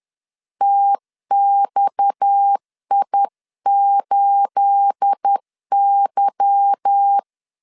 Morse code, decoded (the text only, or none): TXI8Y